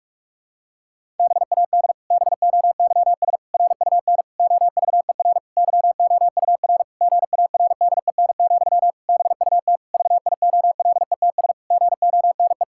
Morse code, heard as a keyed tone, about 46 wpm, 700 Hz. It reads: BAD BOYS RUN OVER YOUR GARDEN, BUT VIOLETS GONE
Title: Pangram Challenge v2 - 45wpm